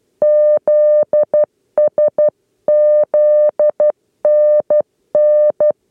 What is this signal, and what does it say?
Morse code, heard as a keyed tone, about 12 wpm, 600 Hz. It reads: ZS ZNN
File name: 8 lezione cw 2